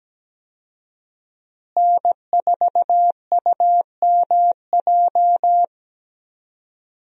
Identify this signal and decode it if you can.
Morse code, heard as a keyed tone, about 17 wpm, 700 Hz. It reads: N4UMJ
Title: USA Call Signs - 5 Characters - Repeated 2x - 17wpm